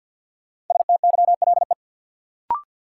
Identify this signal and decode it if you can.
Morse code, heard as a keyed tone, about 51 wpm, 700 Hz. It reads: STYLE
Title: Sentences from Top 800 Words - New Word Drill - 50wpm